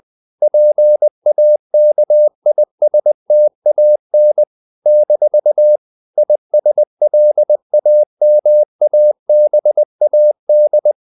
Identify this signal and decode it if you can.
Morse code, heard as a keyed tone, about 20 wpm, 600 Hz. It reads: PAKISTAN - ISLAMABAD
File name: CW / Practice / NationalCapitals20